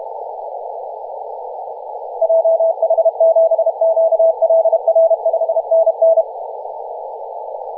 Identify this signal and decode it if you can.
Morse code, heard as a keyed tone, about 31 wpm, 660 Hz, with radio noise. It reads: JH7QLR 5NN
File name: posts / 2016-nov-23-0544z-xx9tkb-24-895mhz-cw